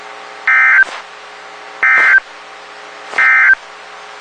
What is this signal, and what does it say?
Emergency Alert System (SAME) tones, heard as harsh buzzing bursts. It is heard from 0.4 to 3.6 s.
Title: WXK91 NOAA Weather Radio